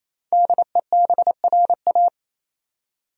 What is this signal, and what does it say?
Morse code, heard as a keyed tone, about 28 wpm, 700 Hz. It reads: DEBRA
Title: USA Common Names - 28wpm